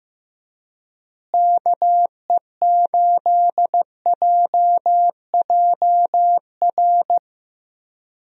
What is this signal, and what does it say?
Morse code, heard as a keyed tone, about 15 wpm, 700 Hz. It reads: KE8JJR
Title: USA Call Signs - 6 Characters - Repeated 3x - 15wpm